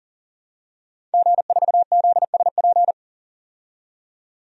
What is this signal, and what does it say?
Morse code, heard as a keyed tone, about 40 wpm, 700 Hz. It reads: G4ZSP
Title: DXCC-100 Call Signs - 5 Characters - 40wpm